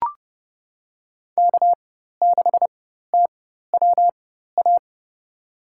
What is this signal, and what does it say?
Morse code, heard as a keyed tone, about 30 wpm, spaced as for 14 wpm, 700 Hz. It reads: K6TWA
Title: USA Call Signs - 5 Characters - 4x Character Spacing - 30wpm